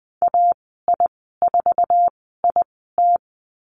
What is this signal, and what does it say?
Morse code, heard as a keyed tone, about 19 wpm, spaced as for 14 wpm, 700 Hz. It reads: AI4IT
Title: 069 Lesson 12g - Letter d - Call Signs 20wpm